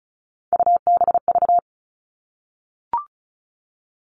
Morse code, heard as a keyed tone, about 34 wpm, 700 Hz. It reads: UBV
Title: All Cut Numbers - 3-Digit 35wpm